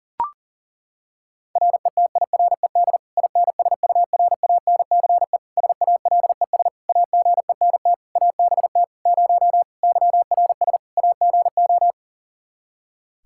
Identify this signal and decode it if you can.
Morse code, heard as a keyed tone, about 37 wpm, 700 Hz. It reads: RETIRED INSURANCE SALES AGENT ABT 0 YRS AGO
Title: Practice QSO - Exchange 3 - More Info - 40wpm